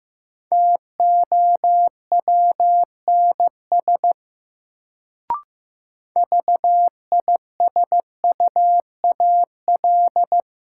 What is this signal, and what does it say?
Morse code, heard as a keyed tone, about 15 wpm, 700 Hz. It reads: TOWNS VISUAL
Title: Sentences from Top 1000 Words - Review Part 6 - New Word Drill - 15wpm